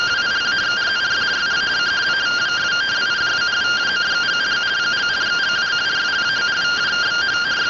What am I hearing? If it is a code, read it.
Two-tone teleprinter signal, about 170 Hz shift.